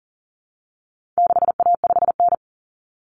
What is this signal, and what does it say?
Morse code, heard as a keyed tone, about 40 wpm, 700 Hz. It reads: 6A5N
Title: Common Cut Numbers - 4-Digit 40wpm